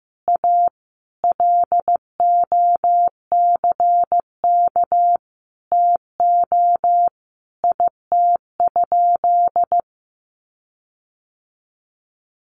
Morse code, heard as a keyed tone, about 15 wpm, 700 Hz. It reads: A LOCK TO IT?
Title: Sentences from Top 1300 Words 15wpm